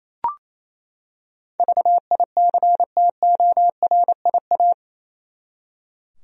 Morse code, heard as a keyed tone, about 28 wpm, 700 Hz. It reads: VICTORIA
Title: USA Common Names - 28wpm